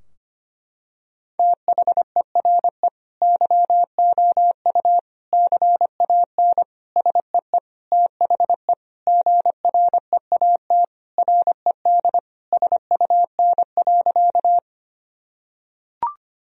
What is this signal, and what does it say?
Morse code, heard as a keyed tone, about 25 wpm, 700 Hz. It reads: THERE YOU CAN SEE THE GREAT RED SUN.
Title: Sentences from Top 1000 Words - Review Part 2 - 25wpm